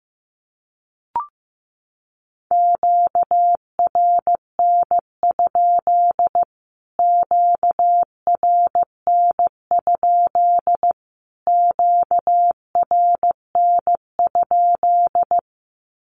Morse code, heard as a keyed tone, about 15 wpm, 700 Hz. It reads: QRN? QRN? QRN?
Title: Q-Codes Repeated 3x 15wpm